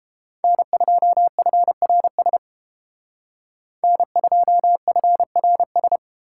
Morse code, heard as a keyed tone, about 29 wpm, 700 Hz. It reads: N2FRS N2FRS